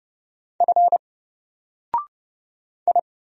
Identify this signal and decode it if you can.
Morse code, heard as a keyed tone, about 30 wpm, 700 Hz. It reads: F I
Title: Single Character (Letter:Number:Punctuation:Prosign) Alt 30wpm